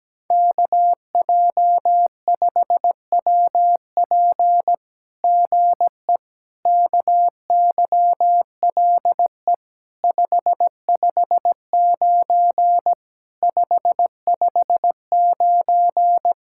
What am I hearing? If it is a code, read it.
Morse code, heard as a keyed tone, about 17 wpm, 700 Hz. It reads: KJ5WP GE KYLE 559 559
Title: SOTA QSO Practice 17wpm